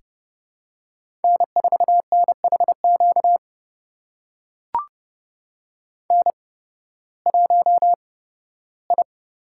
Morse code, heard as a keyed tone, about 30 wpm, spaced as for 16 wpm, 700 Hz. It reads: N4NHQ N1I
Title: USA Call Signs - 5 Characters - 8x Character Spacing - 30wpm